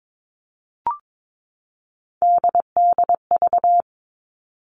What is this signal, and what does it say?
Morse code, heard as a keyed tone, about 22 wpm, 700 Hz. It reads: DDV
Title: All Cut Numbers - 3-Digit 22wpm